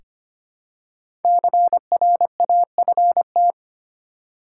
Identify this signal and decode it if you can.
Morse code, heard as a keyed tone, about 25 wpm, 700 Hz. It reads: CRAFT
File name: 5 Letter Words - First 3 Letters Given - 25wpm